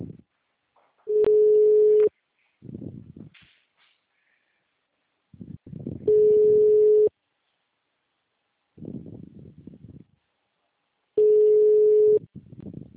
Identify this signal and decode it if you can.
Ringback tone (Europe, 425 Hz).
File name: out- -2024